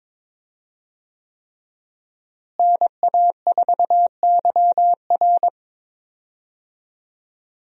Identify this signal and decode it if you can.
Morse code, heard as a keyed tone, about 22 wpm, 700 Hz. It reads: NA4YR
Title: USA Call Signs - 5 Characters - Repeated 3x - 22wpm